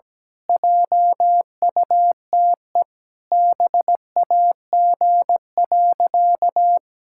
Morse code, heard as a keyed tone, about 17 wpm, 700 Hz. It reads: JUTE BAG.
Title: Pangram Challenge v2 - 17wpm